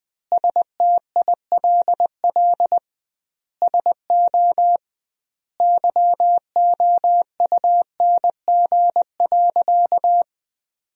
Morse code, heard as a keyed tone, about 21 wpm, 700 Hz. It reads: STILL SO YOUNG.
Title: 2x Word-Spacing - Sentences from Top 400 Words - 20wpm